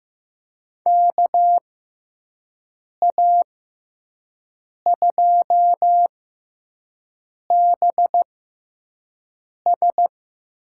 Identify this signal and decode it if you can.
Morse code, heard as a keyed tone, about 15 wpm, spaced as for 5 wpm, 700 Hz. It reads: KA2BS
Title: USA Call Signs - 6x Character Spacing - 15wpm